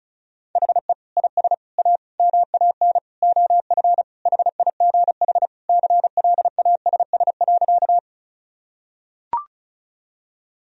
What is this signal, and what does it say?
Morse code, heard as a keyed tone, about 34 wpm, 700 Hz. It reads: HE IS A MAN OF HIGH CLASS.
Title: Sentences from Top 1000 Words - Review Part 2 - 35wpm